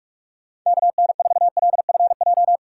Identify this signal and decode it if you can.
Morse code, heard as a keyed tone, about 46 wpm, 700 Hz. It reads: KN4LFJ
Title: USA Call Signs - 6 Characters - 6x Character Spacing - 45wpm